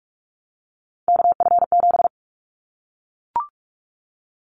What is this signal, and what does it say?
Morse code, heard as a keyed tone, about 44 wpm, 700 Hz. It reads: KF7
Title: Callsign Prefixes for Top 25 DXCC - Repeated 3x - 45wpm